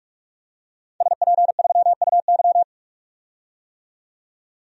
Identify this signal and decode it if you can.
Morse code, heard as a keyed tone, about 41 wpm, 700 Hz. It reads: SP3UY
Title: DXCC-100 Call Signs - 5 Characters - 45wpm